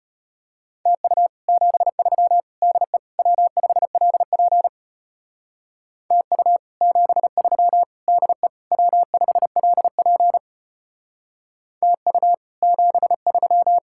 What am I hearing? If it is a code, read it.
Morse code, heard as a keyed tone, about 32 wpm, 700 Hz. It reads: TU 73 DE W5LP TU 73 DE W5LP TU 73